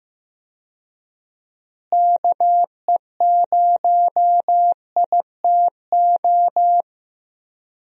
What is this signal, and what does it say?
Morse code, heard as a keyed tone, about 15 wpm, 700 Hz. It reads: KE0ITO
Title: USA Call Signs - 6 Characters - Repeated 2x - 15wpm